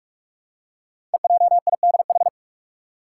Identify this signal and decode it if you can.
Morse code, heard as a keyed tone, about 44 wpm, 700 Hz. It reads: EJIDH
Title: Code Groups - 5 Letters - 45wpm